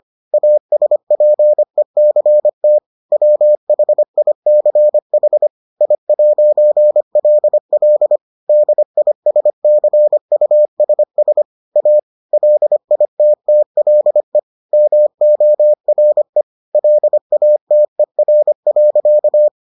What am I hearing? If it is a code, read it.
Morse code, heard as a keyed tone, about 25 wpm, 600 Hz. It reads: ASPECT WHICH I'LL DISCUSS A LITTLE MORE LATER.